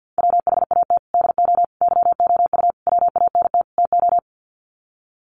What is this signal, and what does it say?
Morse code, heard as a keyed tone, about 50 wpm, 700 Hz. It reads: WHAT DO YOU WANT TO
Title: Sets of 5 Words 50wpm